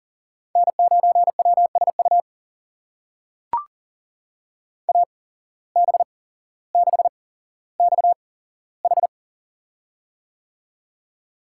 Morse code, heard as a keyed tone, about 40 wpm, spaced as for 17 wpm, 700 Hz. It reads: N9WSU AB6XH
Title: USA Call Signs - 5 Characters - 8x Character Spacing - 40wpm